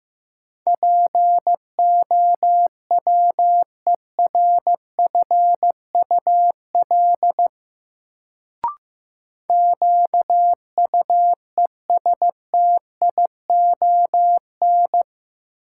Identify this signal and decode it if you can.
Morse code, heard as a keyed tone, about 15 wpm, 700 Hz. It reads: POWERFUL QUESTION
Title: Sets of 5 Words Encore - New Word Drill - 15wpm